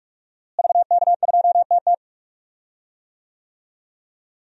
Morse code, heard as a keyed tone, about 46 wpm, 700 Hz. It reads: VK2TT
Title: DXCC-100 Call Signs - 5 Characters - Repeated 3x - 45wpm